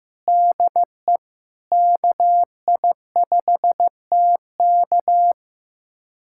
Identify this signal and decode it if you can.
Morse code, heard as a keyed tone, about 15 wpm, 700 Hz. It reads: DE KI5TK
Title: USA Callsign DE Callsign 15wpm